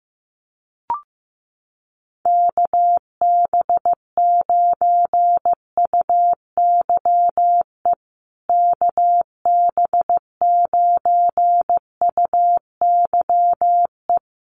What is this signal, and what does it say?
Morse code, heard as a keyed tone, about 15 wpm, 700 Hz. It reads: KB9UYE KB9UYE
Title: USA Call Signs Repeated 3x 15wpm